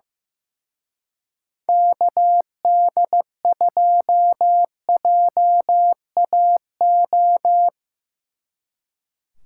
Morse code, heard as a keyed tone, about 17 wpm, 700 Hz. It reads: KD2JAO
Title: Speed Racing - USA 2x3 Call Signs 15wpm